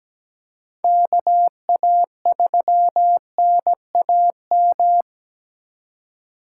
Morse code, heard as a keyed tone, about 17 wpm, 700 Hz. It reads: KA3NAM
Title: K1USN SST QSO Practice - 17wpm